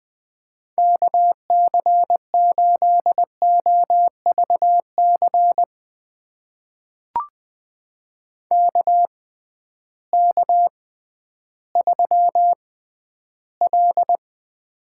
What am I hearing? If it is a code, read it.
Morse code, heard as a keyed tone, about 20 wpm, spaced as for 12 wpm, 700 Hz. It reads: KC8OVC KK3L
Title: USA Call Signs - 6x Character Spacing - 20wpm